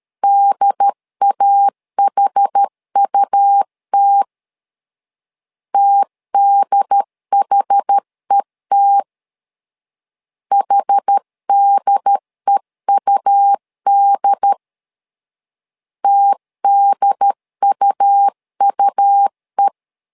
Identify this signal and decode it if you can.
Morse code, heard as a keyed tone, about 13 wpm, 800 Hz. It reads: DAHUT TDHET HDEUD TDUUE
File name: LEZIONI / 2 UHDN / 2 UHD 12wpm 1540